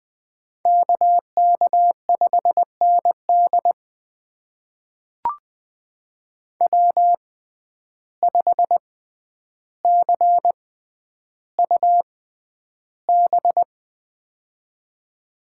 Morse code, heard as a keyed tone, about 20 wpm, spaced as for 10 wpm, 700 Hz. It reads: KK5ND W5CUB
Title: USA Call Signs - 5 Characters - 6x Character Spacing - 20wpm